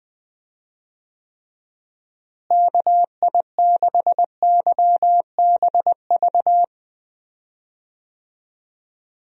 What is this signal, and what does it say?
Morse code, heard as a keyed tone, about 20 wpm, 700 Hz. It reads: KI6YBV
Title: USA Call Signs - 6 Characters - Repeated 3x - 20wpm